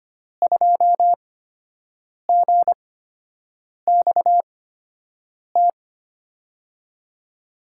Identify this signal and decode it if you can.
Morse code, heard as a keyed tone, about 25 wpm, spaced as for 7 wpm, 700 Hz. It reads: 2GXT
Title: USA Call Signs - 5 Characters - 8x Character Spacing - 25wpm